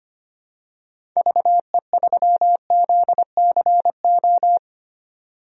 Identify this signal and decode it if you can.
Morse code, heard as a keyed tone, about 25 wpm, 700 Hz. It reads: VE3ZCO
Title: DXCC-100 Call Signs - 6 Characters - 25wpm